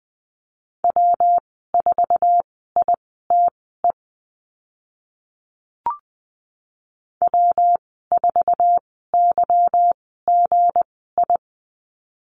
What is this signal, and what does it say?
Morse code, heard as a keyed tone, about 19 wpm, spaced as for 14 wpm, 700 Hz. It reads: W4ITE W4YGI
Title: 173 Lesson 28g - Number 9 - Call Signs 20wpm